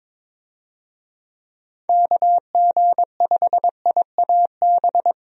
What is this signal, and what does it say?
Morse code, heard as a keyed tone, about 22 wpm, 700 Hz. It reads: KG5IAB
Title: USA Call Signs - 6 Characters - 22wpm